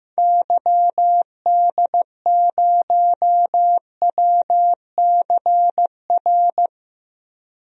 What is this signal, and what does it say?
Morse code, heard as a keyed tone, about 15 wpm, 700 Hz. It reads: YD0WCR
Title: DXCC-100 Call Signs - 6 Characters - Repeated 3x - 15wpm